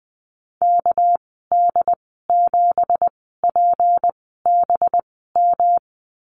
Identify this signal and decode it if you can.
Morse code, heard as a keyed tone, about 20 wpm, spaced as for 15 wpm, 700 Hz. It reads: KD7PBM
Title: 200 Lesson 32g - Letter k - Call Signs 20wpm